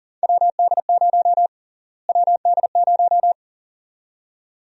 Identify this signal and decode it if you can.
Morse code, heard as a keyed tone, about 40 wpm, 700 Hz. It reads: WD0 WD0
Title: USA Call Sign Prefixes - Repeated 2x - 40wpm